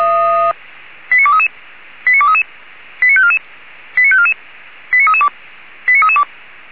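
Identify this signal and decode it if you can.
Five-tone selective call (ZVEI), heard as quick runs of five tones, tones of about 70 ms. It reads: 89233 89233 89544 89544 89332 89332
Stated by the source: ZVEI five-tone selective call 89288, received by SDR, archived by Feuerwehr Orlamünde (feuerwehr-orlamuende) fire brigade.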